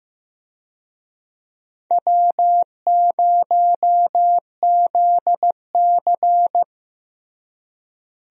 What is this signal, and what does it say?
Morse code, heard as a keyed tone, about 15 wpm, 700 Hz. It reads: W0ZC
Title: USA Call Signs - 4 Characters - Repeated 3x - 15wpm